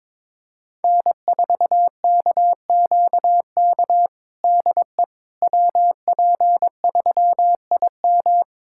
Morse code, heard as a keyed tone, about 22 wpm, 700 Hz. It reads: N4KQK DE WP3IM
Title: USA Callsign DE Callsign - 5 Characters - 22wpm